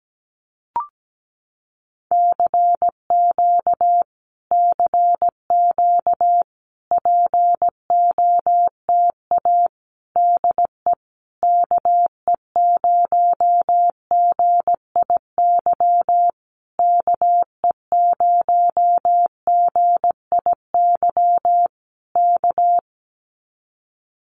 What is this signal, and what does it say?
Morse code, heard as a keyed tone, about 17 wpm, 700 Hz. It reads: CQ CQ POTA DE KE0GIY KE0GIY K
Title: Common QSO Phrases Encore 17wpm